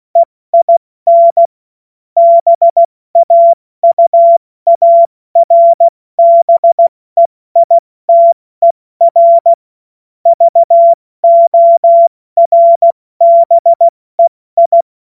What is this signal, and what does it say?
Morse code, heard as a keyed tone, about 16 wpm, 680 Hz. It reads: EIN BAUARBEITER VORBEI
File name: Morsekurs / Witze / Witz 23. Dezember 2025 70 F80